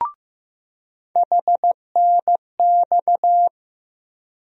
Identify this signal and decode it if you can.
Morse code, heard as a keyed tone, about 15 wpm, 700 Hz. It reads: HNX
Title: USA Call Sign Postfixes - 3 Letters 15wpm